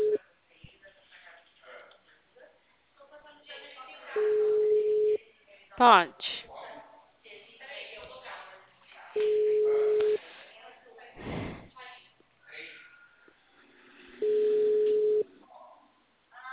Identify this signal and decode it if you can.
Ringback tone (Europe, 425 Hz), with speech.
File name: out- -2001- - - .13380